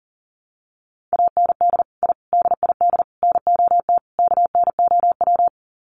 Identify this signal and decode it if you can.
Morse code, heard as a keyed tone, about 40 wpm, 700 Hz. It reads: AND I DID NOT KNOW
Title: Sets of 5 Words Encore 40wpm